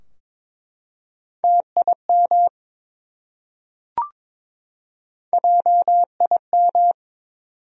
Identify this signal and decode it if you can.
Morse code, heard as a keyed tone, about 22 wpm, 700 Hz. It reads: TIM JIM
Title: USA Common Names - Max 3 Letters - 22wpm